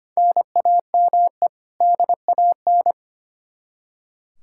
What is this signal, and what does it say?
Morse code, heard as a keyed tone, about 25 wpm, 700 Hz. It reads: NAME DAN
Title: Practice QSO - Exchange 1 - Beginning - 25wpm